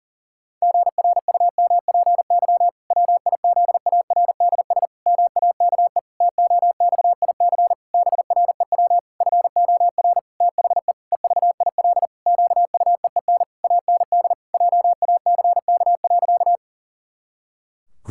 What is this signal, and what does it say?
Morse code, heard as a keyed tone, about 40 wpm, 700 Hz. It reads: GRUMPY WIZARDS MAKE TOXIC BREW FOR THE EVIL QUEEN AND JACK.
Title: Pangram Challenge v2 - 40wpm